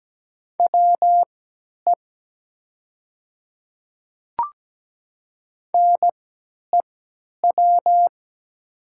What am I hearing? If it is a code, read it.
Morse code, heard as a keyed tone, about 17 wpm, spaced as for 10 wpm, 700 Hz. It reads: WE NEW
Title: 3x Character Spacing - Top 200 Words - 17wpm